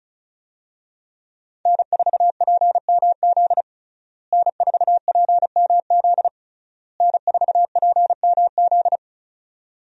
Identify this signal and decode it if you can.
Morse code, heard as a keyed tone, about 36 wpm, 700 Hz. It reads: N4PMZ N4PMZ N4PMZ
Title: USA Call Signs - 5 Characters - Repeated 3x - 35wpm